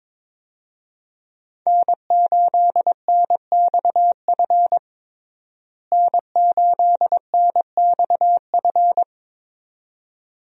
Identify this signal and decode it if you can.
Morse code, heard as a keyed tone, about 21 wpm, 700 Hz. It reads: N8NXF N8NXF
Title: USA Call Signs - 5 Characters - Repeated 2x - 22wpm